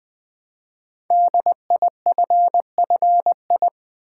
Morse code, heard as a keyed tone, about 20 wpm, 700 Hz. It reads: DIFFI
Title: Build A Word - 5 Letters At A Time - 10-Letter Words - 20wpm